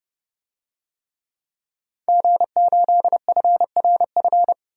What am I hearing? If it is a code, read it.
Morse code, heard as a keyed tone, about 32 wpm, 700 Hz. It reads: G8FRF
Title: DXCC-100 Call Signs - 5 Characters - 30wpm